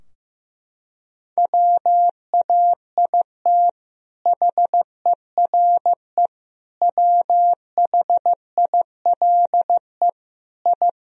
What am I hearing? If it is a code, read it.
Morse code, heard as a keyed tone, about 15 wpm, 700 Hz. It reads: WAIT HERE WHILE I
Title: Sentences from Top 1000 Words - Review Part 4 - 15wpm